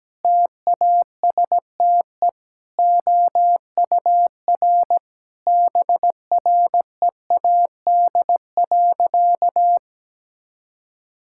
Morse code, heard as a keyed tone, about 17 wpm, 700 Hz. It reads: TASTE OUR BREAD.
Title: Sentences from Top 1600 Words 17wpm